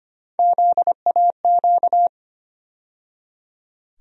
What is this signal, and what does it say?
Morse code, heard as a keyed tone, about 25 wpm, 700 Hz. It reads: ZAQ
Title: USA Call Sign Postfixes - 3 Letters 25wpm